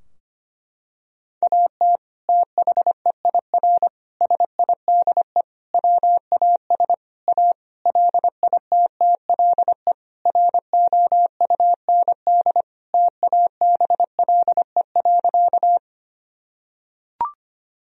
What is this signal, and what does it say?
Morse code, heard as a keyed tone, about 25 wpm, 700 Hz. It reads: AT THEIR SIDE WAS A LITTLE ROUND TABLE.
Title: Sentences from Top 1000 Words - Review Part 2 - 25wpm